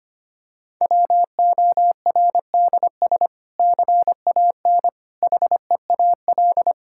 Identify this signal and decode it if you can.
Morse code, heard as a keyed tone, about 25 wpm, 700 Hz. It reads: WORDS CAN HEAL
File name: practice / 25wpm / Difficult Sentences with Hints - 25wpm